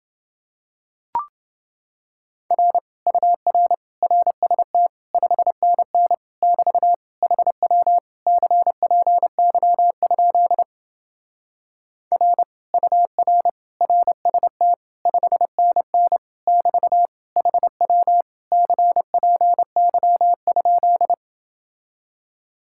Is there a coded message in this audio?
Yes: Morse code from 2.5 to 21.1 s, heard as a keyed tone, about 28 wpm, 700 Hz. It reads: R UR RST 5NN <BT> HW CPY? R UR RST 5NN <BT> HW CPY?